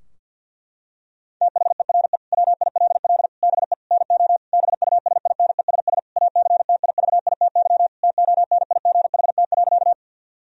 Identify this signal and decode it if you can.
Morse code, heard as a keyed tone, about 52 wpm, 700 Hz. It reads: THERE WILL BE NO BUSINESS ACTIVITY TONIGHT.
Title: Sentences from Top 1000 Words - Review Part 4 - 50wpm